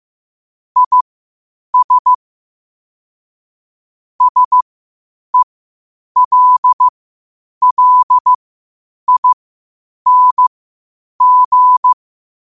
Morse code, heard as a keyed tone, about 15 wpm, spaced as for 8 wpm, 1000 Hz. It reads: IS SELLING
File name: qotdfiles / 8wpm